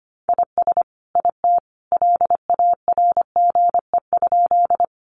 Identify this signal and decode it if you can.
Morse code, heard as a keyed tone, about 25 wpm, 700 Hz. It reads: IS IT LARGE?
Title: Sentences from Top 200 Words 25wpm